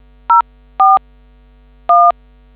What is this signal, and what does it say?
DTMF, heard as telephone touch tones, tones of about 200 ms. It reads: 04 1